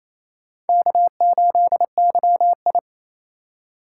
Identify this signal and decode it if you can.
Morse code, heard as a keyed tone, about 27 wpm, 700 Hz. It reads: K8YI
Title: USA Call Signs - 4 Characters - 3x Character Spacing - 28wpm